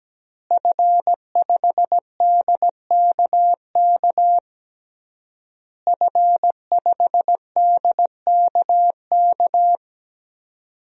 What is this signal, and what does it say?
Morse code, heard as a keyed tone, about 17 wpm, 700 Hz. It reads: F5DKK F5DKK
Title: DXCC-100 Call Signs - 5 Characters - Repeated 3x - 17wpm